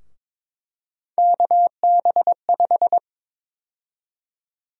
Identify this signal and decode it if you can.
Morse code, heard as a keyed tone, about 22 wpm, 700 Hz. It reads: KB5
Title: USA Call Sign Prefixes - Repeated 2x - 22wpm